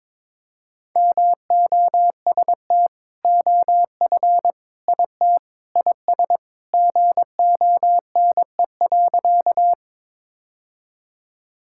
Morse code, heard as a keyed tone, about 22 wpm, 700 Hz. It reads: MOST OF IT IS GONE.
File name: practice / 22wpm / Sentences with 2-Word Phrases - 22wpm